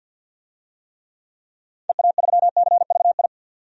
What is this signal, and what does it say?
Morse code, heard as a keyed tone, about 52 wpm, 700 Hz. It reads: EA3CVI
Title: DXCC-100 Call Signs - 6 Characters - 50wpm